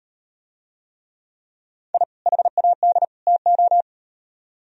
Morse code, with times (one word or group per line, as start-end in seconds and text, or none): 1.9-2.0 I
2.3-3.0 HAD
3.3-3.8 TO